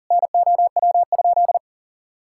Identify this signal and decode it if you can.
Morse code, heard as a keyed tone, about 40 wpm, 700 Hz. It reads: NOW?
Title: Sentences from Top 1600 Words 40wpm